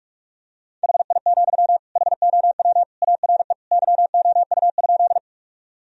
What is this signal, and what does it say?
Morse code, heard as a keyed tone, about 46 wpm, 700 Hz. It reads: HI, HOW ARE YOU?
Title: Sentences from Top 1000 Words - Review Part 3 - 45wpm